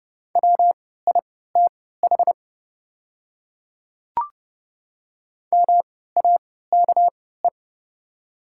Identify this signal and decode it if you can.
Morse code, heard as a keyed tone, about 30 wpm, spaced as for 17 wpm, 690 Hz. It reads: WITH MAKE
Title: 3x Character Spacing - Top 100 Words - 30wpm